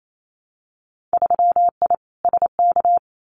Morse code, heard as a keyed tone, about 28 wpm, 700 Hz. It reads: 3I SK
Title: Winter Field Day Exchanges Encore 28wpm